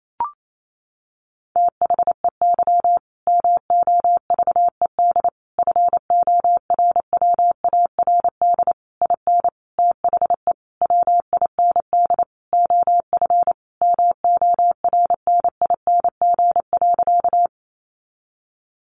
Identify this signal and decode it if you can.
Morse code, heard as a keyed tone, about 28 wpm, 700 Hz. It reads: THEY MOVED FORWARD IN THE WIND OF MORNING.
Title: A Sound Of Thunder 28wpm